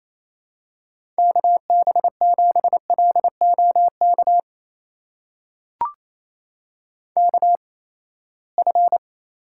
Morse code, heard as a keyed tone, about 27 wpm, spaced as for 19 wpm, 700 Hz. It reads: KB7LOK KF
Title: USA Call Signs - 6 Characters - 8x Character Spacing - 28wpm